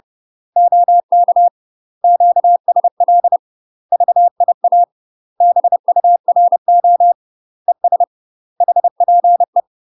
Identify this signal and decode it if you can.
Morse code, heard as a keyed tone, about 30 wpm, 700 Hz. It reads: OK QSL VIA BURO ES HPE
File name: UK MASTER QSOs TRAINING / 30WPM / UK MASTER QSOs 0033